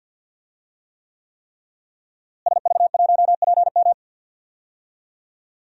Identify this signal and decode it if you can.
Morse code, heard as a keyed tone, about 48 wpm, 700 Hz. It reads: SV1PM
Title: DXCC-100 Call Signs - 5 Characters - Repeated 3x - 50wpm